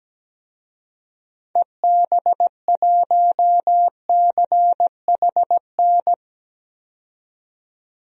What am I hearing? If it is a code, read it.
Morse code, heard as a keyed tone, about 17 wpm, 700 Hz. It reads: EB1CHN
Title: DXCC-100 Call Signs - 6 Characters - 17wpm